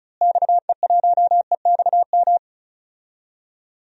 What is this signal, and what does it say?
Morse code, heard as a keyed tone, about 34 wpm, 700 Hz. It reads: XE1EXM
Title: DXCC-100 Call Signs - 6 Characters - Repeated 3x - 35wpm